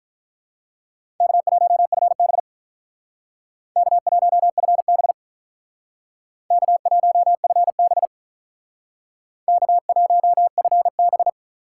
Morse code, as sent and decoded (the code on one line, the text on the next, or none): -.- .---- ..-. -... / -.- .---- ..-. -... / -.- .---- ..-. -... / -.- .---- ..-. -...
K1FB K1FB K1FB K1FB